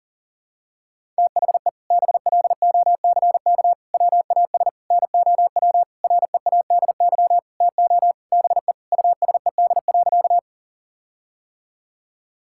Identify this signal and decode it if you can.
Morse code, heard as a keyed tone, about 38 wpm, 700 Hz. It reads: THE BLOCK WAS NOW READY TO BE USED.
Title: Sentences from Top 1600 Words 40wpm